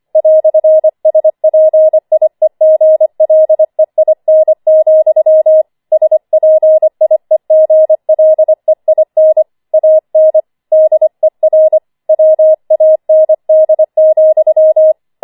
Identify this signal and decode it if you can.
Morse code, heard as a keyed tone, about 25 wpm, 630 Hz. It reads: "SPIEGLEIN, SPIEGLEIN AN DER WAND,
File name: 25wpm